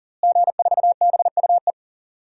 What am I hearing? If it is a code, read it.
Morse code, heard as a keyed tone, about 40 wpm, 700 Hz. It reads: G4BUE
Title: DXCC-100 Call Signs - 5 Characters - Repeated 3x - 40wpm